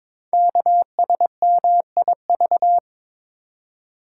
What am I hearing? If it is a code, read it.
Morse code, heard as a keyed tone, about 22 wpm, 700 Hz. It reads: KSMIV
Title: Code Groups - 5 Letters - 22wpm